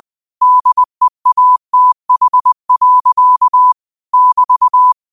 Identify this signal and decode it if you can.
Morse code, heard as a keyed tone, about 20 wpm, 1000 Hz. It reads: DEATH. <BT>
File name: qotdfiles / 20wpm